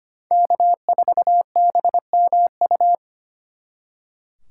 Morse code, heard as a keyed tone, about 32 wpm, 700 Hz. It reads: K4BMU